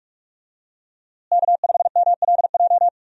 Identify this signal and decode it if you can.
Morse code, heard as a keyed tone, about 46 wpm, 700 Hz. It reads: K5MLJ
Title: USA Call Signs - 5 Characters - Repeated 2x - 45wpm